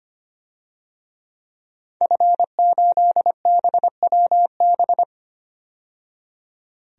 Morse code, heard as a keyed tone, about 25 wpm, 700 Hz. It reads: F8BWB
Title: DXCC-100 Call Signs - 5 Characters - 25wpm